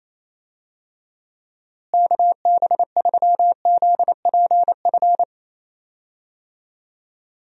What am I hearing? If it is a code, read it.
Morse code, heard as a keyed tone, about 28 wpm, 700 Hz. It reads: KB3ZPF
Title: USA Call Signs - 6 Characters - Repeated 3x - 28wpm